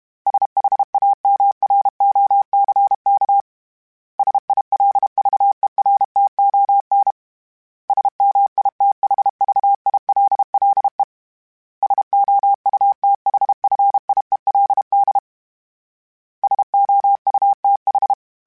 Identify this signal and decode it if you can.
Morse code, heard as a keyed tone, about 32 wpm, 800 Hz. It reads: SHAMROCK SILVERTON SMITHVILLE SOUTHFIELD SOUTH
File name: fromSMs / FromSMs mp3s / Cities@30